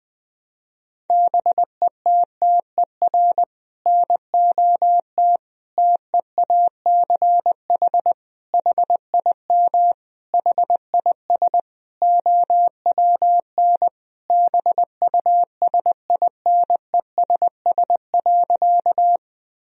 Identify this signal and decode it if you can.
Morse code, heard as a keyed tone, about 20 wpm, 700 Hz. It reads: BETTER NOT TEACH HIM HIS OWN BUSINESS.
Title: Sentences from Top 1000 Words - Review Part 2 - 20wpm